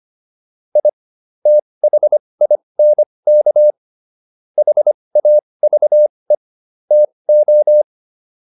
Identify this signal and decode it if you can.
Morse code, heard as a keyed tone, about 25 wpm, spaced as for 20 wpm, 600 Hz. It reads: I THINK HAVE TO